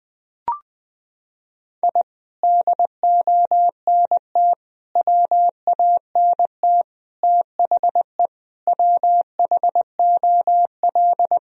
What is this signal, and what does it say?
Morse code, heard as a keyed tone, about 20 wpm, 700 Hz. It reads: I DONT WANT THE WHOL
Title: Sentences from Top 1000 Words - Review Part 2 - 20wpm